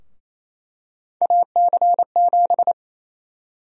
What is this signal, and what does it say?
Morse code, heard as a keyed tone, about 27 wpm, 700 Hz. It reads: AC7
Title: USA Call Sign Prefixes - 28wpm